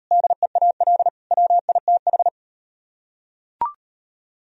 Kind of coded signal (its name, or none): Morse code